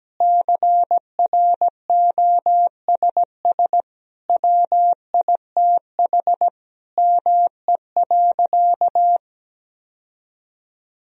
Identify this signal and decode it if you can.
Morse code, heard as a keyed tone, about 17 wpm, 700 Hz. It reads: CROSS WITH ME.
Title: Sentences from Top 1000 Words - Review Part 1 - 17wpm